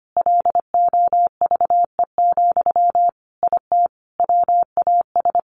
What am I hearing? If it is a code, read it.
Morse code, heard as a keyed tone, about 25 wpm, 700 Hz. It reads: LOVE, IT WAS
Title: Sentences from Top 1400 Words 25wpm